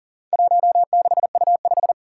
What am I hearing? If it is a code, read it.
Morse code, heard as a keyed tone, about 40 wpm, 700 Hz. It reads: 16U5
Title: Four Challenging Letter-Number Combinations 40wpm